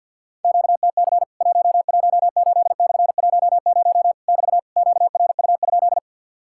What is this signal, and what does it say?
Morse code, heard as a keyed tone, about 50 wpm, 700 Hz. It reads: QTC 118/10 <BT> QRU?
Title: Worked All Europe DX Contest - QTC Exchange with Repeats - 50wpm